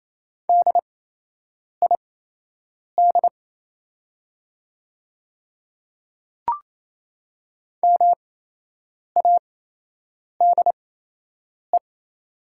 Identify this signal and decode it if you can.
Morse code, heard as a keyed tone, about 28 wpm, spaced as for 8 wpm, 700 Hz. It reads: DID MADE
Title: 8x Character Spacing - Top 300 Words - 28wpm